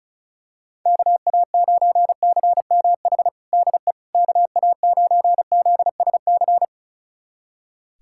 Morse code, heard as a keyed tone, about 34 wpm, 700 Hz. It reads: KA9CMH DE KA9ZSC
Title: USA Callsign DE Callsign 35wpm